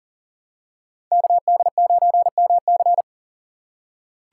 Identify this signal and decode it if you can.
Morse code, heard as a keyed tone, about 40 wpm, 700 Hz. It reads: KD9MC
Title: USA Call Signs - 5 Characters - 6x Character Spacing - 40wpm